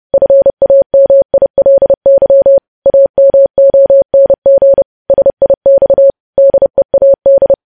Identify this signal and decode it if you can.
Morse code, heard as a keyed tone, about 30 wpm, 550 Hz. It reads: FAMILY AMONG SIX DEAD
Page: news headlines in morse code at 30 wpm